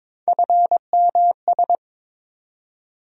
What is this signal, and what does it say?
Morse code, heard as a keyed tone, about 22 wpm, 700 Hz. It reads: FMS
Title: USA Call Sign Postfixes - 3 Letters 22wpm